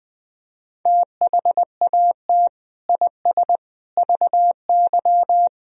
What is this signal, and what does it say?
Morse code, heard as a keyed tone, about 20 wpm, 700 Hz. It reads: THAT IS VY
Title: Sentences from Top 400 Words and Abbreviations 20wpm